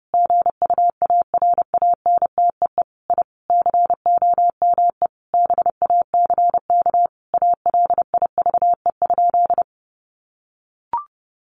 Morse code, heard as a keyed tone, about 30 wpm, 700 Hz. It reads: GUARANTEE I COME BACK ALIVE?
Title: A Sound Of Thunder 30wpm